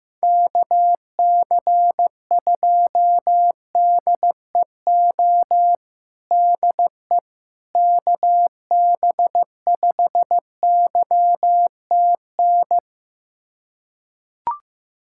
Morse code, heard as a keyed tone, about 15 wpm, 700 Hz. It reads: KC2DEO DE KB5YTN
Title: USA Callsign DE Callsign 15wpm